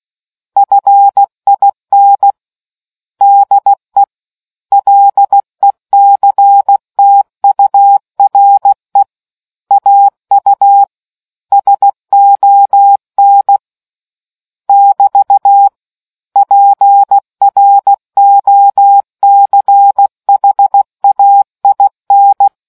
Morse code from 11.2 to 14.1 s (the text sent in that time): SON